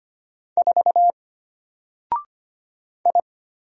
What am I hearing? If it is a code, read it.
Morse code, heard as a keyed tone, about 25 wpm, 700 Hz. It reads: 4 I
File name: Single-Letter-Number-25wpm